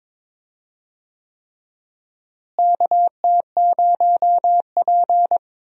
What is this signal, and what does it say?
Morse code, heard as a keyed tone, about 22 wpm, 700 Hz. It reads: KT0P
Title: USA Call Signs - 4 Characters - Repeated 3x - 22wpm